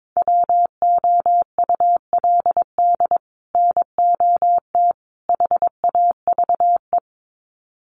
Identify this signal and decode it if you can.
Morse code, heard as a keyed tone, about 22 wpm, 700 Hz. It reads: WOULD NOT HAVE
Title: Sets of 3 Words 22wpm